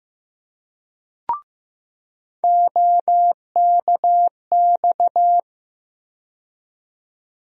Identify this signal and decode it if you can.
Morse code, heard as a keyed tone, about 15 wpm, 700 Hz. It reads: OKX
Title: USA Call Sign Postfixes - 3 Letters 15wpm